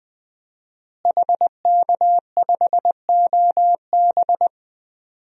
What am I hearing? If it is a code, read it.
Morse code, heard as a keyed tone, about 20 wpm, 700 Hz. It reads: HK5OB
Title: DXCC-100 Call Signs - 5 Characters - Repeated 2x - 20wpm